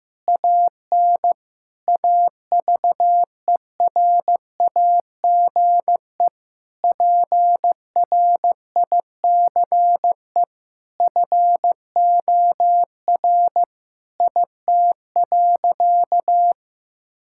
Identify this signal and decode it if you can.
Morse code, heard as a keyed tone, about 15 wpm, 700 Hz. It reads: AN AVERAGE PRICE FOR IT.
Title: Sentences from Top 1000 Words - Review Part 6 - 15wpm